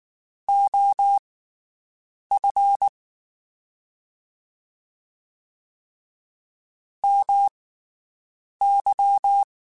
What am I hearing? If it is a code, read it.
Morse code, heard as a keyed tone, about 19 wpm, spaced as for 7 wpm, 800 Hz. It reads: OF MY
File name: QSOs Spaced